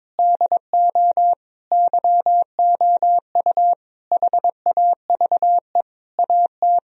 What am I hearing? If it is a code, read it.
Morse code, heard as a keyed tone, about 22 wpm, 700 Hz. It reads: DO YOU HAVE AT
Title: Sentences from Top 1000 Words - Review Part 4 - 22wpm